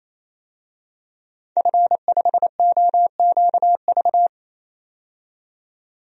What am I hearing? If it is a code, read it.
Morse code, heard as a keyed tone, about 28 wpm, 700 Hz. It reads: F5OQV
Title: DXCC-100 Call Signs - 5 Characters - 28wpm